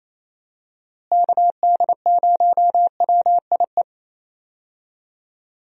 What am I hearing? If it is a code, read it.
Morse code, heard as a keyed tone, about 28 wpm, 700 Hz. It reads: KD0WIE
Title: USA Call Signs - 6 Characters - Repeated 2x - 28wpm